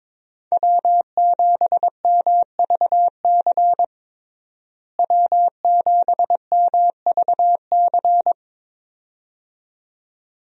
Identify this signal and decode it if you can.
Morse code, heard as a keyed tone, about 22 wpm, 700 Hz. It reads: W7MVC W7MVC
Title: USA Call Signs - 5 Characters - Repeated 2x - 22wpm